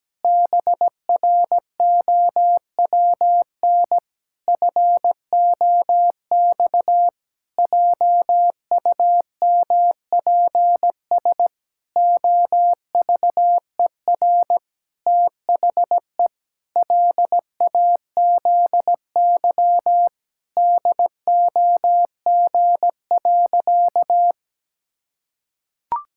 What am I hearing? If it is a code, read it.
Morse code, heard as a keyed tone, about 17 wpm, 700 Hz. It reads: BROWN FOX JUMPS OVER THE LAZY DOG.
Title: Pangram Challenge v2 - 17wpm